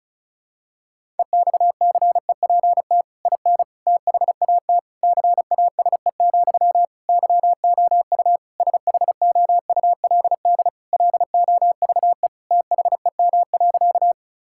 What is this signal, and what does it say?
Morse code, heard as a keyed tone, about 36 wpm, 700 Hz. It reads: EXCEPT IN THAT CASE, YOU SHOULD LOVE THEM.
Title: Sentences from Top 1600 Words 35wpm